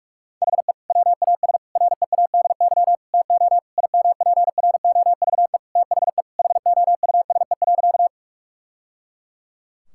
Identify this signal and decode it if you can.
Morse code, heard as a keyed tone, about 42 wpm, 700 Hz. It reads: HE WAS READY TO IMPROVE THE HOUSE.
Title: Sentences from Top 1000 Words - Review Part 5 - 45wpm